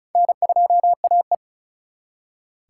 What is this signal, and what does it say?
Morse code, heard as a keyed tone, about 48 wpm, spaced as for 41 wpm, 700 Hz. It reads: N2AE
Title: Speed Racing - USA 1x2 Call Signs 35wpm